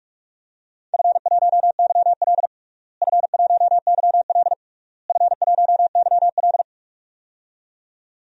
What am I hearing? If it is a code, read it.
Morse code, heard as a keyed tone, about 46 wpm, 690 Hz. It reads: F1YL F1YL F1YL
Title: DXCC-100 Call Signs - 4 Characters - Repeated 3x - 45wpm